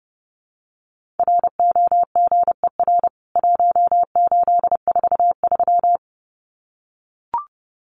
Morse code, heard as a keyed tone, about 30 wpm, 700 Hz. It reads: ROGER 1843
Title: CWT Exchange 30wpm